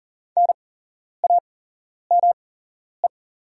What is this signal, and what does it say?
Morse code, heard as a keyed tone, about 40 wpm, spaced as for 11 wpm, 690 Hz. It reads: NAME
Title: 8x Character Spacing - Top 200 Words - 40wpm